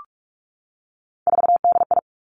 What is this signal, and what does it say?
Morse code, heard as a keyed tone, about 46 wpm, 700 Hz. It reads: 4DI
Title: Three Challenging Letter-Number Combinations 45wpm